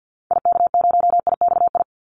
Morse code, heard as a keyed tone, about 50 wpm, 700 Hz. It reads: IK0IXI
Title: Call Signs Repeated 3x - 50wpm